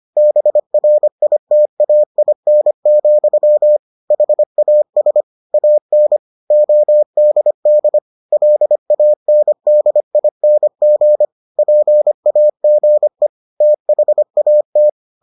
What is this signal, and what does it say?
Morse code, heard as a keyed tone, about 25 wpm, 600 Hz. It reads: BRITAIN, HAS AN ODD LANDING PAGE THAT